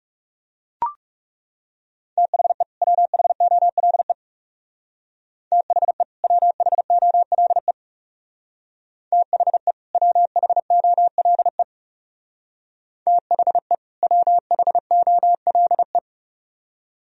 Morse code, heard as a keyed tone, about 36 wpm, 700 Hz. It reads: THE WHOLE THE WHOLE THE WHOLE THE WHOLE
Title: Speed-Racing - Sets of 2 Words 30wpm